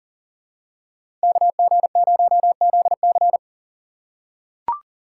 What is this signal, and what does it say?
Morse code, heard as a keyed tone, about 40 wpm, 700 Hz. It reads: KG0ZC
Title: USA Call Signs - 5 Characters - 4x Character Spacing - 40wpm